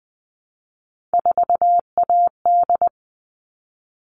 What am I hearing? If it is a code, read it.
Morse code, heard as a keyed tone, about 20 wpm, 700 Hz. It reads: 4AD
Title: All Cut Numbers - 3-Digit 20wpm